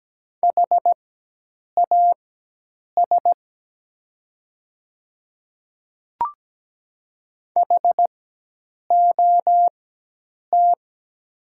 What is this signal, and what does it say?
Morse code, heard as a keyed tone, about 17 wpm, spaced as for 8 wpm, 700 Hz. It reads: HAS HOT
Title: 4x Character Spacing - Top 200 Words - 17wpm